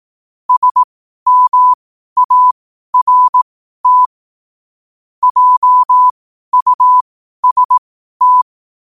Morse code, heard as a keyed tone, about 18 wpm, spaced as for 12 wpm, 1000 Hz. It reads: SMART JUST
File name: qotdfiles / 12wpm